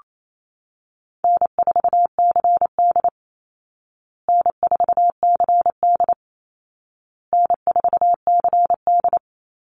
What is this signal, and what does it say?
Morse code, heard as a keyed tone, about 28 wpm, 700 Hz. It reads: N4CD N4CD N4CD
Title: Easy Call Signs Repeated 3x - 28wpm